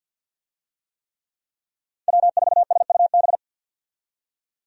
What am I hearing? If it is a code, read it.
Morse code, heard as a keyed tone, about 50 wpm, 700 Hz. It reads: W4SUB
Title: USA Call Signs - 5 Characters - 50wpm